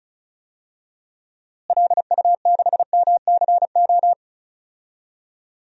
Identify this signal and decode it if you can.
Morse code, heard as a keyed tone, about 36 wpm, 700 Hz. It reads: LU6MCO
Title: DXCC-100 Call Signs - 6 Characters - Repeated 2x - 35wpm